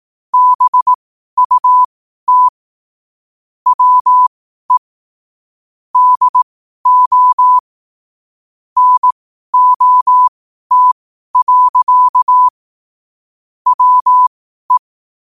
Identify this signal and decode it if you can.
Morse code, heard as a keyed tone, about 18 wpm, spaced as for 12 wpm, 1000 Hz. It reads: BUT WE DO NOT. WE
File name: qotdfiles / 12wpm